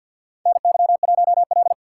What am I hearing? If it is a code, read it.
Morse code, heard as a keyed tone, about 52 wpm, 700 Hz. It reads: NY1L
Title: USA Call Signs - 4 Characters - 2x Character Spacing - 50wpm